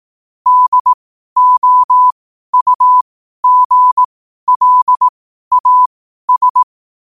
Morse code, heard as a keyed tone, about 18 wpm, spaced as for 12 wpm, 1000 Hz. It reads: DOUGLAS
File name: qotdfiles / 12wpm